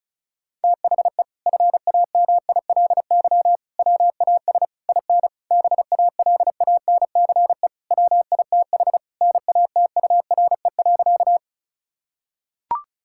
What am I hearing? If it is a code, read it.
Morse code, heard as a keyed tone, about 36 wpm, 700 Hz. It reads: THE FAMILY WAS IN BALANCE WITH NATURE.
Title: Sentences from Top 1000 Words - Review Part 6 - 35wpm